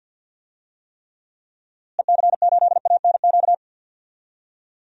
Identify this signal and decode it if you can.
Morse code, heard as a keyed tone, about 47 wpm, 700 Hz. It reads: EC8ANQ
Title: DXCC-100 Call Signs - 6 Characters - Repeated 2x - 50wpm